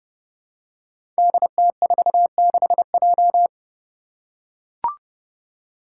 Morse code, heard as a keyed tone, about 30 wpm, 700 Hz. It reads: DT46J
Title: Five Challenging Letter-Number Combinations 30wpm